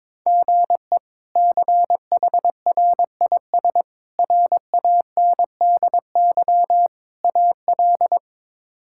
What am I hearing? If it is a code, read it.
Morse code, heard as a keyed tone, about 22 wpm, 700 Hz. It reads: GE CHRIS RANDY AL
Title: K1USN SST QSO Practice - 22wpm